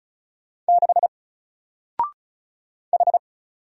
Morse code, heard as a keyed tone, about 36 wpm, 700 Hz. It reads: 6 H
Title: Single Letter-Number Alt 35wpm